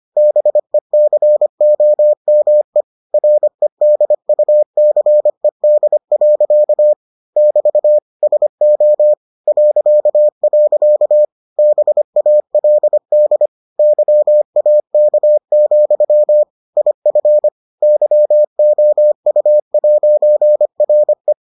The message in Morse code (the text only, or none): BECOME REDUCED. <BT> SO .. BALD YAK, IF YOU'RE